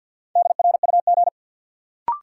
Morse code, heard as a keyed tone, about 54 wpm, 700 Hz. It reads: DRUG
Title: Sets of 5 Words - New Word Drill - 50wpm